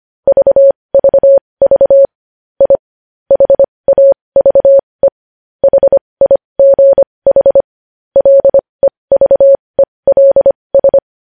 Morse code, heard as a keyed tone, about 25 wpm, spaced as for 20 wpm, 550 Hz. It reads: VVV I HAVE HIGH LEVELS
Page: News Headlines in Morse Code at 20 WPM